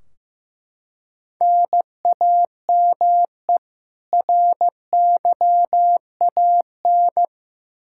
Morse code, heard as a keyed tone, about 15 wpm, 700 Hz. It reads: NAME RYAN
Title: Practice QSO - Exchange 1 - Beginning - 15wpm